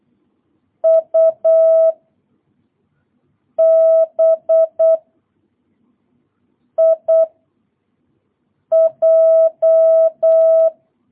Morse code, heard as a keyed tone, about 8 wpm, 660 Hz. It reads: U B I J